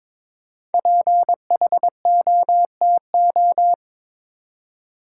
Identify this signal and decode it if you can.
Morse code, heard as a keyed tone, about 22 wpm, 700 Hz. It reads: PHOTO
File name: 5 Letter Words - First 3 Letters Given - 22wpm